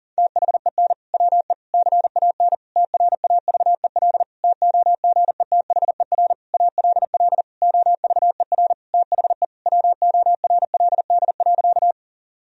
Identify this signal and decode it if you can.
Morse code, heard as a keyed tone, about 40 wpm, 700 Hz. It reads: THEN WE CAN TRAVEL TOGETHER ALL OVER THE WORLD.
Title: Sentences from Top 1000 Words - Review Part 2 - 40wpm